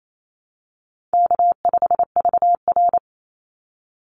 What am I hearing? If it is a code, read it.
Morse code, heard as a keyed tone, about 28 wpm, 700 Hz. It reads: K5VR
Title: Call Signs Repeated 2x - 28wpm